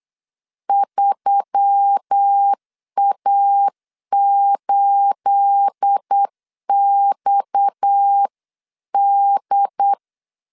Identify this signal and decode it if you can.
Morse code, heard as a keyed tone, about 8 wpm, 800 Hz. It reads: 3A8XD
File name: CW / DET / 1 9 1500C S C 8wpm 5653